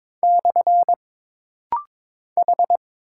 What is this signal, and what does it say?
Morse code, heard as a keyed tone, about 22 wpm, 700 Hz. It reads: / H
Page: This rapid-fire timing is geared to push your ICR (Instant Character Recognition) skills.